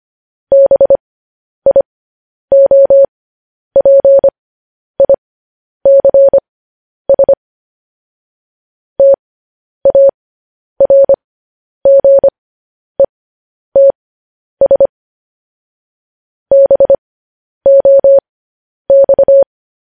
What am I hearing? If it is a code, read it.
Morse code, heard as a keyed tone, about 25 wpm, spaced as for 10 wpm, 550 Hz. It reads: BIOPICS TARGETS BOX
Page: news headlines in morse code at 10 wpm